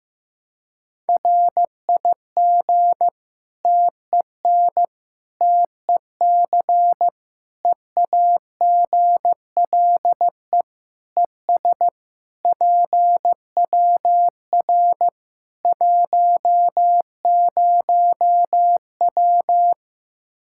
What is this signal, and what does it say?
Morse code, heard as a keyed tone, about 15 wpm, 700 Hz. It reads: RIG TEN TEC EAGLE ES PWR 10W
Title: K1USN SST QSO Practice 15wpm